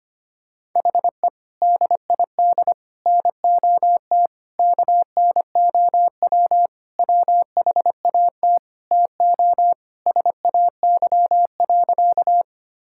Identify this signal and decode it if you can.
Morse code, heard as a keyed tone, about 25 wpm, 700 Hz. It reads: HE DID NOT KNOW WHAT TO SAY.
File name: practice / 25wpm / Sentences with 3-Word Phrases - 25wpm